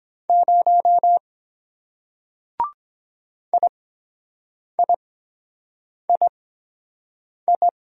Morse code, heard as a keyed tone, about 23 wpm, spaced as for 7 wpm, 700 Hz. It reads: 0 IIII